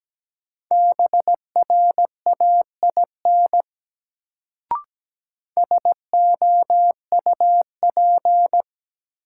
Morse code, heard as a keyed tone, about 17 wpm, 700 Hz. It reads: BRAIN SOUP
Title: Sentences from Top 1000 Words - Review Part 6 - New Word Drill - 17wpm